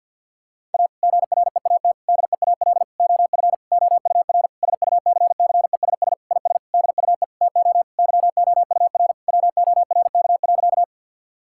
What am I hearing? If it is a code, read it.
Morse code, heard as a keyed tone, about 50 wpm, 700 Hz. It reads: A GREAT DEAL OF OUR SUCCESS IS DUE TO YOUR WORK.
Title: Sentences from Top 1000 Words - Review Part 6 - 50wpm